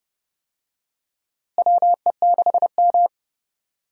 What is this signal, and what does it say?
Morse code, heard as a keyed tone, about 30 wpm, 700 Hz. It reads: WE6M
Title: USA Call Signs - 4 Characters - Repeated 3x - 30wpm